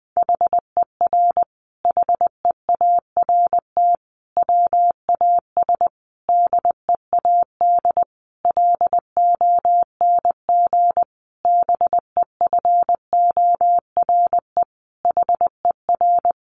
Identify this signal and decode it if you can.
Morse code, heard as a keyed tone, about 20 wpm, 700 Hz. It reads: HER HEART WAS DEAD LONG BEFORE HER
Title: Sentences from Top 1500 Words 20wpm